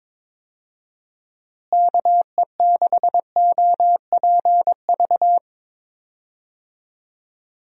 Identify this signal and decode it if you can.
Morse code, heard as a keyed tone, about 22 wpm, 700 Hz. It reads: KE6OPV
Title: USA Call Signs - 6 Characters - Repeated 2x - 22wpm